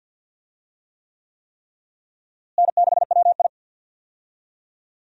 Morse code, heard as a keyed tone, about 52 wpm, 700 Hz. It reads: N6WI
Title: USA Call Signs - 4 Characters - Repeated 3x - 50wpm